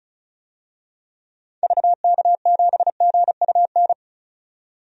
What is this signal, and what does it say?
Morse code, heard as a keyed tone, about 36 wpm, 700 Hz. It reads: VK7GUN
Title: DXCC-100 Call Signs - 6 Characters - 35wpm